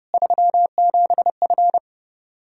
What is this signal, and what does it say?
Morse code, heard as a keyed tone, about 30 wpm, 700 Hz. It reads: 37F
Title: Practice QSO - Exchange 2 - Info - 30wpm